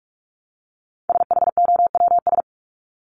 Morse code, heard as a keyed tone, about 44 wpm, 700 Hz. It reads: SHOWS
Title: ES:S Suffix - Top 100 Words with Base Words 45wpm